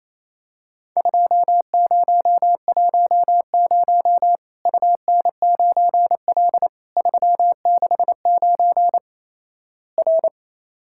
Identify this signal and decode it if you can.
Morse code, heard as a keyed tone, about 28 wpm, 700 Hz. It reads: 2010 UN9L 369 R
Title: Worked All Europe DX Contest - QTC Exchange - 28wpm